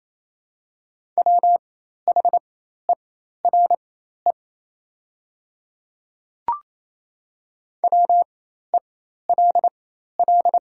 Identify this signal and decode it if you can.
Morse code, heard as a keyed tone, about 28 wpm, spaced as for 13 wpm, 700 Hz. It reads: WHERE WELL
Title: 4x Character Spacing - Top 200 Words - 28wpm